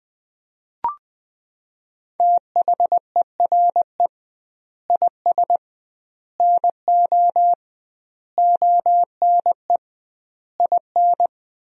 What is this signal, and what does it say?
Morse code, heard as a keyed tone, about 20 wpm, 700 Hz. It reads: THERE IS NO ONE IN
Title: 2x Word-Spacing - Sentences from Top 500 Words - 20wpm